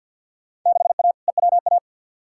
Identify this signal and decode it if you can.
Morse code, heard as a keyed tone, about 50 wpm, 700 Hz. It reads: 6A EWA
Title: Field Day Exchanges Encore - 50wpm